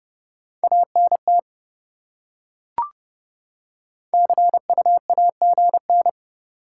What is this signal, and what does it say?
Morse code, heard as a keyed tone, about 29 wpm, 700 Hz. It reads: ANT CUAGN